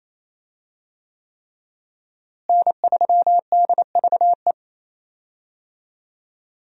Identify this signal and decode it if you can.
Morse code, heard as a keyed tone, about 28 wpm, 700 Hz. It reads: N3DVE
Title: USA Call Signs - 5 Characters - Repeated 3x - 28wpm